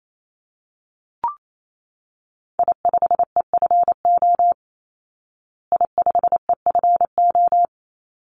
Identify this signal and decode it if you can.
Morse code, heard as a keyed tone, about 28 wpm, 700 Hz. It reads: I5EFO I5EFO
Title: Call Signs Repeated 3x - 28wpm